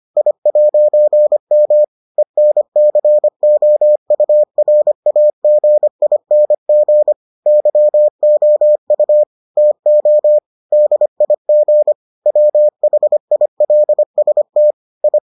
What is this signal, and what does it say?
Morse code, heard as a keyed tone, about 25 wpm, 600 Hz. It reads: I'M ENCOURAGING YOU TO DIG WHILST I